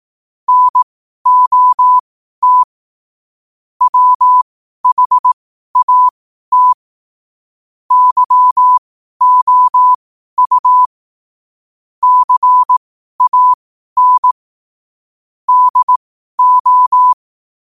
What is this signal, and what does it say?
Morse code, heard as a keyed tone, about 18 wpm, spaced as for 12 wpm, 1000 Hz. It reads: NOT WHAT YOU CAN DO
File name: qotdfiles / 12wpm